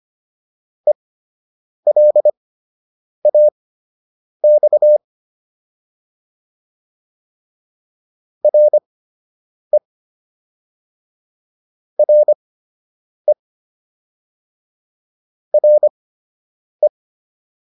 Morse code, heard as a keyed tone, about 25 wpm, spaced as for 8 wpm, 600 Hz. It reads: ELAX RE RE RE